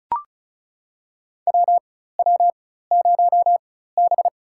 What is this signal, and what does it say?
Morse code, heard as a keyed tone, about 36 wpm, spaced as for 16 wpm, 700 Hz. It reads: WW0B
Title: USA Call Signs - 4 Characters - 4x Character Spacing - 35wpm